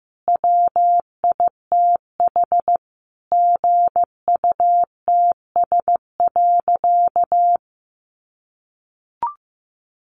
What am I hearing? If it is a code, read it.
Morse code, heard as a keyed tone, about 15 wpm, 700 Hz. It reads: WITH GUTS.
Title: A Sound Of Thunder 15wpm